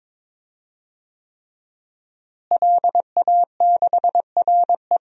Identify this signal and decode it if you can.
Morse code, heard as a keyed tone, about 21 wpm, 700 Hz. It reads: LA6RE
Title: DXCC-100 Call Signs - 5 Characters - Repeated 3x - 22wpm